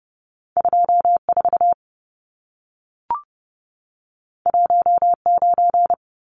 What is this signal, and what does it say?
Morse code, heard as a keyed tone, about 30 wpm, 700 Hz. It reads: 24 19
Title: Numbers - 2-Digit 30wpm